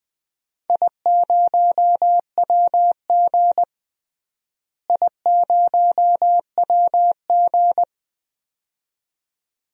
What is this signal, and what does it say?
Morse code, heard as a keyed tone, about 21 wpm, 700 Hz. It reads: I0WG I0WG
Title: DXCC-100 Call Signs - 4 Characters - Repeated 2x - 20wpm